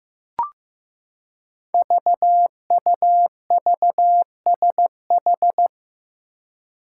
Morse code, heard as a keyed tone, about 15 wpm, 700 Hz. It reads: VUVSH
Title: Five Challenging Letter-Number Combinations 15wpm